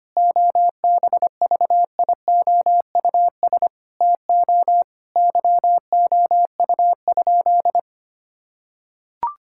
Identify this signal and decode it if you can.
Morse code, heard as a keyed tone, about 25 wpm, 700 Hz. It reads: OBVIOUS TO YOU?
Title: Sentences from Top 1600 Words 25wpm